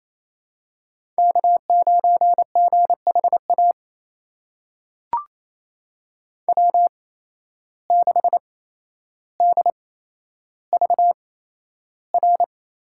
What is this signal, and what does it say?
Morse code, heard as a keyed tone, about 28 wpm, spaced as for 12 wpm, 700 Hz. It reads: K9GHA W6DVR
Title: USA Call Signs - 5 Characters - 8x Character Spacing - 28wpm